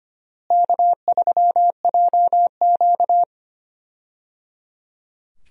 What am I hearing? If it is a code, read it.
Morse code, heard as a keyed tone, about 25 wpm, 700 Hz. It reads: K3JQ
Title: USA Call Signs - 4 Characters - Repeated 3x - 25wpm